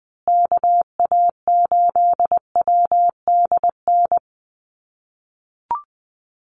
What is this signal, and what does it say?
Morse code, heard as a keyed tone, about 20 wpm, 700 Hz. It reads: KA8WDN
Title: USA Call Signs Repeated 3x 20wpm